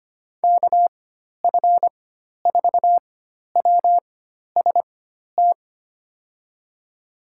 Morse code, heard as a keyed tone, about 25 wpm, spaced as for 12 wpm, 700 Hz. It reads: KF4WST
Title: USA Call Signs - 6 Characters - 4x Character Spacing - 25wpm